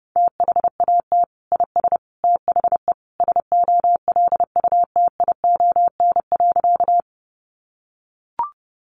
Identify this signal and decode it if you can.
Morse code, heard as a keyed tone, about 30 wpm, 700 Hz. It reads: THAT IS THE SOLUTION.
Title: Sentences from Top 1300 Words 30wpm